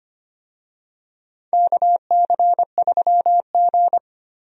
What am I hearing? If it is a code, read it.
Morse code, heard as a keyed tone, about 25 wpm, 700 Hz. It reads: KC3G
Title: USA Call Signs - 4 Characters - Repeated 3x - 25wpm